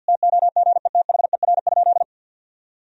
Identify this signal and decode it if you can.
Morse code, heard as a keyed tone, about 51 wpm, 700 Hz. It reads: TOGETHER?
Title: Sentences from Top 1000 Words - Review Part 2 - 50wpm